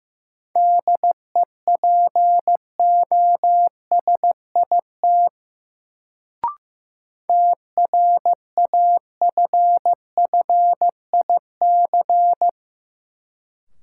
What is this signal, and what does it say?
Morse code, heard as a keyed tone, about 15 wpm, 700 Hz. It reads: DEPOSIT TRAFFIC
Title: Sentences from Top 1300 Words - New Word Drill - 15wpm